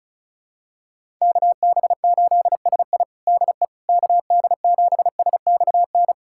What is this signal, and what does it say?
Morse code, heard as a keyed tone, about 36 wpm, 700 Hz. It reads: KB8SI DE KD7SXN
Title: USA Callsign DE Callsign 35wpm